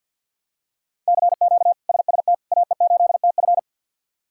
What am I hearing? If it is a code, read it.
Morse code, heard as a keyed tone, about 51 wpm, 700 Hz. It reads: CQ SST AE8TF
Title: K1USN SST QSO Practice - 50wpm